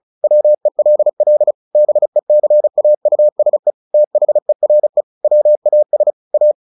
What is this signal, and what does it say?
Morse code, heard as a keyed tone, about 34 wpm, 600 Hz. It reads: WELL BECAUSE THERE WAS A
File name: the old man and the sea / the old man and the sea 1 35wpm